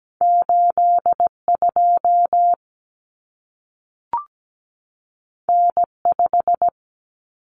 Morse code, heard as a keyed tone, about 17 wpm, 700 Hz. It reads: 82 N5
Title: Common Cut Numbers - 2-Digit 17wpm